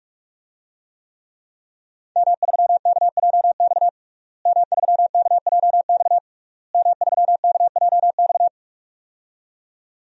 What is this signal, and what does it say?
Morse code, heard as a keyed tone, about 48 wpm, 700 Hz. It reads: M3KJX M3KJX M3KJX
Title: DXCC-100 Call Signs - 5 Characters - Repeated 3x - 45wpm